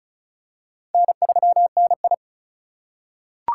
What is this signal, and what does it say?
Morse code, heard as a keyed tone, about 36 wpm, 700 Hz. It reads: N3NI
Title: USA Call Signs - 4 Characters - 2x Character Spacing - 35wpm